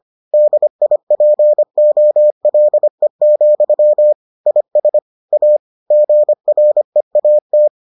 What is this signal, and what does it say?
Morse code, heard as a keyed tone, about 25 wpm, 600 Hz. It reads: DIPOLE, IS A GREAT